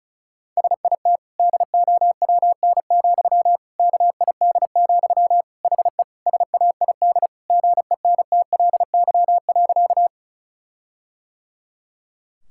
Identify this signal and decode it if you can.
Morse code, heard as a keyed tone, about 36 wpm, 700 Hz. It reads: SIT DOWN, KID, HE SAID GENTLY.
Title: Sentences from Top 1600 Words 35wpm